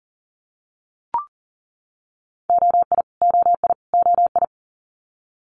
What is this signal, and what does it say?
Morse code, heard as a keyed tone, about 38 wpm, 700 Hz. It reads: OI OI OI
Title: Multi-letter Phonemes Repeated 3x 40wpm